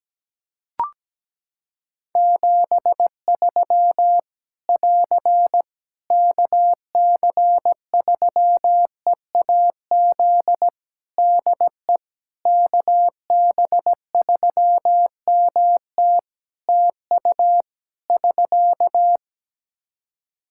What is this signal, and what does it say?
Morse code, heard as a keyed tone, about 17 wpm, 700 Hz. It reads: 73 <AR> KC3EAZ DE KB3MT TU <SK>
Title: Practice QSO - Exchange 4 - Ending - 17wpm